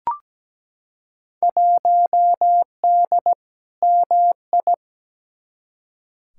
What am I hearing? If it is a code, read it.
Morse code, heard as a keyed tone, about 17 wpm, 700 Hz. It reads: 1D MI
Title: Field Day Exchanges - 17wpm